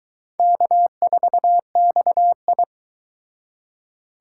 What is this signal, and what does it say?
Morse code, heard as a keyed tone, about 24 wpm, 700 Hz. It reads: K4XI